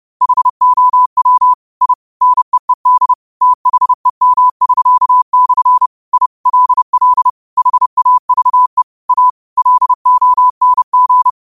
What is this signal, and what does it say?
Morse code, heard as a keyed tone, about 30 wpm, 1000 Hz. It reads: HOW I NEED THEM<SK>/ I LL HAVE A LONG
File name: qotdfiles / 30wpm